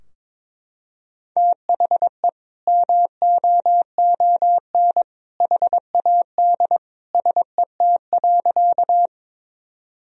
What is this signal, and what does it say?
Morse code, heard as a keyed tone, about 22 wpm, 700 Hz. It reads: THE MOON HAD SET.
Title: Sentences from Top 1000 Words - Review Part 3 - 22wpm